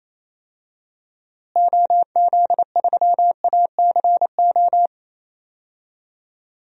Morse code, heard as a keyed tone, about 28 wpm, 700 Hz. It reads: OZ3ACO
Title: DXCC-100 Call Signs - 6 Characters - Repeated 3x - 28wpm